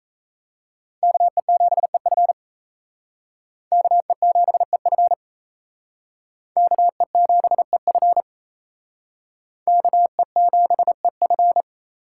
Morse code, heard as a keyed tone, about 35 wpm, 700 Hz. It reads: KE7EF KE7EF KE7EF KE7EF
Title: Speed Racing - USA 2x2 Call Signs 28wpm